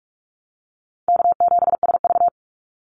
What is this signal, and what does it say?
Morse code, heard as a keyed tone, about 44 wpm, 700 Hz. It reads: K7SV
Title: Easy Call Signs Repeated 3x - 45wpm